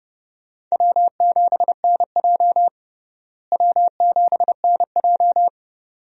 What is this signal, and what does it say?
Morse code, heard as a keyed tone, about 30 wpm, 700 Hz. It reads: W7NJ W7NJ
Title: USA Call Signs - 4 Characters - Repeated 2x - 30wpm